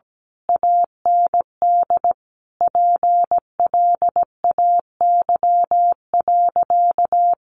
Morse code, heard as a keyed tone, about 17 wpm, 700 Hz. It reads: AND PLAY.
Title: Sentences from Top 200 Words 17wpm